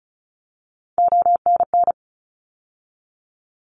Morse code, heard as a keyed tone, about 36 wpm, 700 Hz. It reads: ONN
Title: ARRL Sections 35wpm